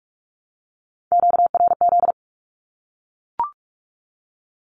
Morse code, heard as a keyed tone, about 44 wpm, 700 Hz. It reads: QRZ
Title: Q-Codes 45wpm